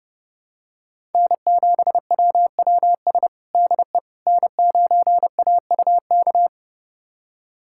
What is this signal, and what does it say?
Morse code, heard as a keyed tone, about 30 wpm, 700 Hz. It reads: N7WWS DE N9AUK
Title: USA Callsign DE Callsign - 5 Characters - 30wpm